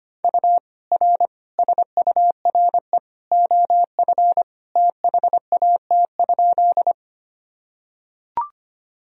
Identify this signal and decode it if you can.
Morse code, heard as a keyed tone, about 25 wpm, 700 Hz. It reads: U R SURE OF THAT?
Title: Sentences from Top 300 Words and Abbreviations 25wpm